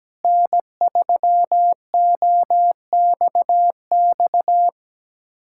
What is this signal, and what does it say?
Morse code, heard as a keyed tone, about 17 wpm, 700 Hz. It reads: N3OXX
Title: USA Callsign DE Callsign 17wpm